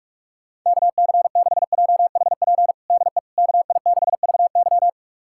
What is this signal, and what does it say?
Morse code, heard as a keyed tone, about 46 wpm, 700 Hz. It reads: KC6JHP DE KI6VY
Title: USA Callsign DE Callsign 45wpm